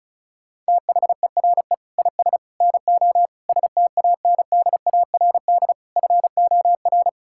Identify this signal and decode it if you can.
Morse code, heard as a keyed tone, about 36 wpm, 700 Hz. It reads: THERE IS NO STANDARD FOR
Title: Sentences from Top 1600 Words 35wpm